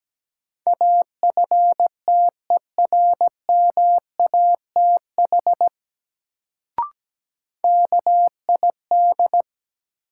Morse code, heard as a keyed tone, about 17 wpm, 700 Hz. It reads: AFTERMATH KID
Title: Sets of 5 Words - New Word Drill - 17wpm